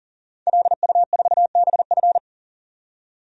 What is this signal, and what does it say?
Morse code, heard as a keyed tone, about 40 wpm, 700 Hz. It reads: LU4BF
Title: DXCC-100 Call Signs - 5 Characters - 40wpm